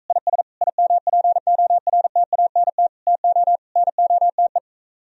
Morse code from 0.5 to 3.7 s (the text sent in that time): IMPORTANT TO